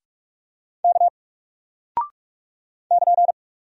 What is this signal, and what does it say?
Morse code, heard as a keyed tone, about 44 wpm, 700 Hz. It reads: K <KN>
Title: Single Character (Letter:Number:Punctuation:Prosign) Alt 45wpm